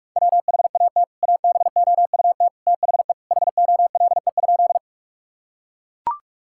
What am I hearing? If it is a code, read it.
Morse code, heard as a keyed tone, about 44 wpm, 700 Hz. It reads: WHAT ABOUT THE HOLE?
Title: Sentences from Top 1600 Words 45wpm